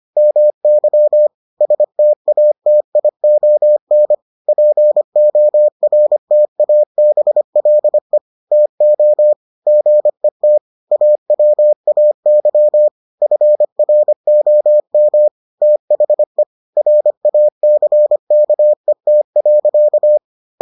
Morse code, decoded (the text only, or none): MY STATION PORTABLE TO GET AWAY FROM THE RACKET.